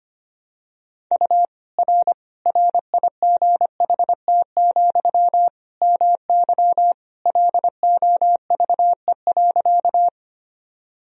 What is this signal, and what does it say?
Morse code, heard as a keyed tone, about 25 wpm, 700 Hz. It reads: U R RIGHT, MY LOVE.
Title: Sentences from Top 400 Words and Abbreviations 25wpm